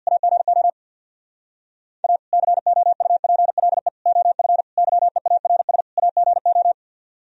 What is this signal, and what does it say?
Morse code, heard as a keyed tone, about 64 wpm, 700 Hz. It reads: AGO A COUPLE OF YEARS AGO